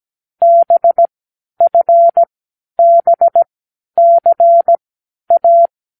Morse code, heard as a keyed tone, about 17 wpm, spaced as for 10 wpm, 680 Hz. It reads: BFBCA
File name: k7qo / Lesson9(A-F)